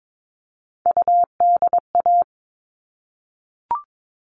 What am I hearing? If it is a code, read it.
Morse code, heard as a keyed tone, about 22 wpm, 700 Hz. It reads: UDA
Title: All Cut Numbers - 3-Digit 22wpm